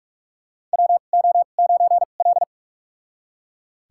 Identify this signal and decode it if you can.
Morse code, heard as a keyed tone, about 41 wpm, spaced as for 31 wpm, 690 Hz. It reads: WO9L
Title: USA Call Signs - 4 Characters - 2x Character Spacing - 45wpm